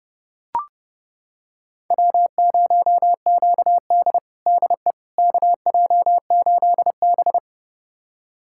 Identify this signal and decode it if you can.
Morse code, heard as a keyed tone, about 30 wpm, 700 Hz. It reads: W0QD DE KJ8B
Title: USA Callsign DE Callsign - 4 Characters - 30wpm